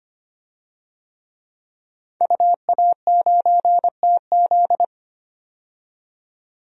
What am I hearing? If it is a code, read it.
Morse code, heard as a keyed tone, about 25 wpm, 700 Hz. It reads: UA9TZ
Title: DXCC-100 Call Signs - 5 Characters - 25wpm